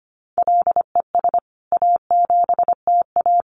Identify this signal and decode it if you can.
Morse code, heard as a keyed tone, about 25 wpm, 700 Hz. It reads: LES A7TA
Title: CWT Exchange 25wpm